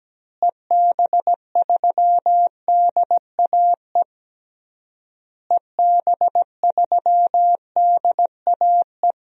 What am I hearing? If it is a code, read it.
Morse code, heard as a keyed tone, about 17 wpm, 700 Hz. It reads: EB3DAE EB3DAE
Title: DXCC-100 Call Signs - 6 Characters - Repeated 3x - 17wpm